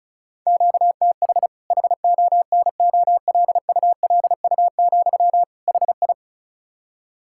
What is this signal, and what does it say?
Morse code, heard as a keyed tone, about 36 wpm, 700 Hz. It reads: QTH HONOLULU, HI
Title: K1USN SST QSO Practice 35wpm